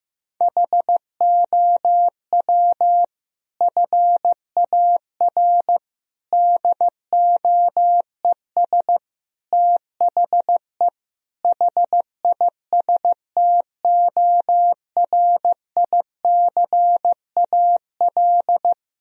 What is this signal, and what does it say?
Morse code, heard as a keyed tone, about 15 wpm, 700 Hz. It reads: HOW FAR DOES THE HISTORICAL
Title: Sentences from Top 1000 Words - Review Part 4 - 15wpm